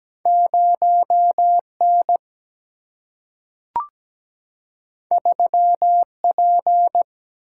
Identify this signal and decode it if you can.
Morse code, heard as a keyed tone, about 17 wpm, 700 Hz. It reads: 0 N 3 P
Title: Single-Letter Single-Number 17wpm